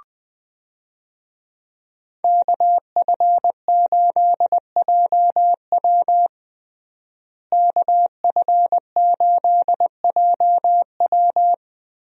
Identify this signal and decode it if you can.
Morse code, heard as a keyed tone, about 21 wpm, 700 Hz. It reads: KF8JW KF8JW
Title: USA Call Signs - 5 Characters - Repeated 2x - 20wpm